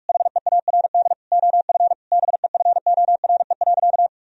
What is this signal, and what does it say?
Morse code, heard as a keyed tone, about 46 wpm, 700 Hz. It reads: HEARD OF BEFORE.
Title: Sentences from Top 1000 Words - Review Part 3 - 45wpm